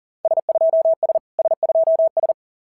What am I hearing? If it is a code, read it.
Morse code, heard as a keyed tone, about 41 wpm, 650 Hz. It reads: S2S S2S
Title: SOTA QSO Practice 40wpm